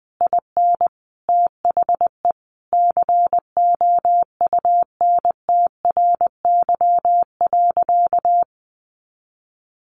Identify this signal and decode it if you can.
Morse code, heard as a keyed tone, about 20 wpm, 700 Hz. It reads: IN THE COUNTRY.
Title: Sentences from Top 700 Words 20wpm